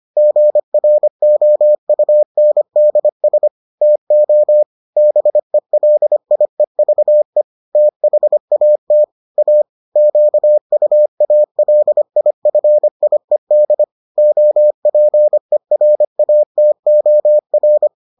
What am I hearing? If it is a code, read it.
Morse code, heard as a keyed tone, about 25 wpm, 600 Hz. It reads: GROUNDS TO BELIEVE THAT A QUALIFIED OPERATOR